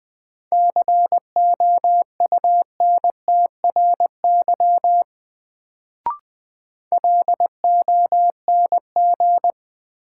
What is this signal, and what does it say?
Morse code, heard as a keyed tone, about 20 wpm, 700 Hz. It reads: COUNTRY LONG
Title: Sets of 2 Words - New Word Drill - 20wpm